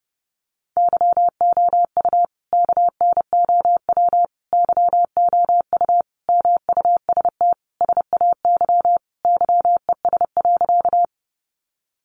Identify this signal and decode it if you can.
Morse code, heard as a keyed tone, about 30 wpm, 700 Hz. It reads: YOU KNOW YOU MUST SAY YES.
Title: Sentences from Top 500 Words 30wpm